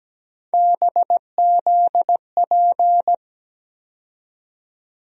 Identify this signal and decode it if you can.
Morse code, heard as a keyed tone, about 17 wpm, 700 Hz. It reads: BZP
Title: USA Call Sign Postfixes - 3 Letters 17wpm